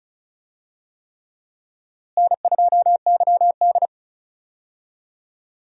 Morse code, heard as a keyed tone, about 34 wpm, 700 Hz. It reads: N2YD
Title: USA Call Signs - 4 Characters - Repeated 3x - 35wpm